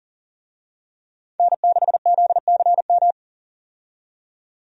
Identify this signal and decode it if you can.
Morse code, heard as a keyed tone, about 40 wpm, 700 Hz. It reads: N6ZCM
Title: USA Call Signs - 5 Characters - 40wpm